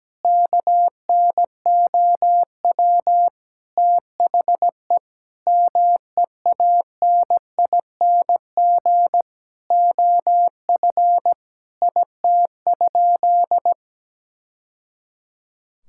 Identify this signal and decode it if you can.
Morse code, heard as a keyed tone, about 17 wpm, 700 Hz. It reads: KNOW THE MEANING OF IT?
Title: Sentences from Top 1000 Words - Review Part 6 - 17wpm